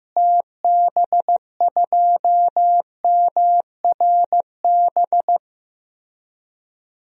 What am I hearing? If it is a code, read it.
Morse code, heard as a keyed tone, about 15 wpm, 700 Hz. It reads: TB2MRB
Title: DXCC-100 Call Signs - 6 Characters - Repeated 2x - 15wpm